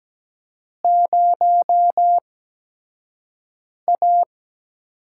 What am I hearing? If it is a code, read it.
Morse code, heard as a keyed tone, about 17 wpm, spaced as for 5 wpm, 700 Hz. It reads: 0A
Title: USA Call Signs - 4 Characters - 8x Character Spacing - 17wpm